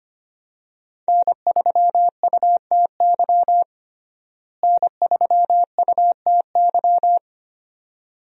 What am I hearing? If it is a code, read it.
Morse code, heard as a keyed tone, about 25 wpm, 700 Hz. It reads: N3UTY N3UTY
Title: USA Call Signs - 5 Characters - Repeated 2x - 25wpm